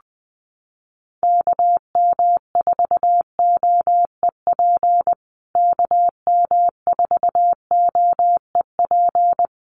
Morse code, heard as a keyed tone, about 20 wpm, 700 Hz. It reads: KM4OEP KM4OEP
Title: USA Call Signs Repeated 3x 20wpm